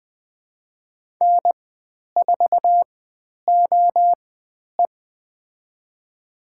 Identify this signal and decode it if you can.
Morse code, heard as a keyed tone, about 20 wpm, spaced as for 10 wpm, 700 Hz. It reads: N4OE
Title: 095 Lesson 16g - Letter c - Call Signs 20wpm